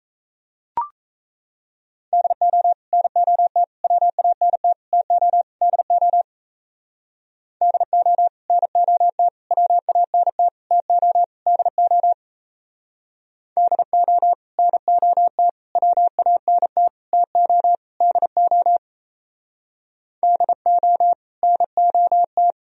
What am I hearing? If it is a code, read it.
Morse code, heard as a keyed tone, about 34 wpm, 700 Hz. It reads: DO NOT WANT TO DO DO NOT WANT TO DO DO NOT WANT TO DO DO NOT
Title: Speed-Racing - Sets of 5 Words 28wpm